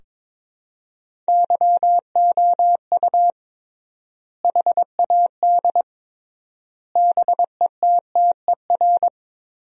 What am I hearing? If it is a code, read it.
Morse code, heard as a keyed tone, about 22 wpm, 700 Hz. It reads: YOU HAD BETTER
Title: 3x Word-Spacing - Sentences from Top 400 Words - 22wpm